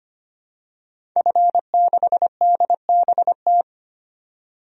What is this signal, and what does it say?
Morse code, heard as a keyed tone, about 25 wpm, 700 Hz. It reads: F6DBT
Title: DXCC-100 Call Signs - 5 Characters - 25wpm